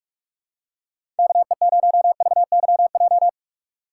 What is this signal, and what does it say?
Morse code, heard as a keyed tone, about 46 wpm, 690 Hz. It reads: KE0VYJ
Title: USA Call Signs - 6 Characters - 3x Character Spacing - 45wpm